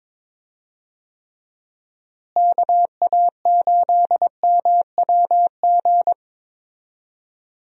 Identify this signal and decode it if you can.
Morse code, heard as a keyed tone, about 22 wpm, 700 Hz. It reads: KA8MWG
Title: USA Call Signs - 6 Characters - Repeated 2x - 22wpm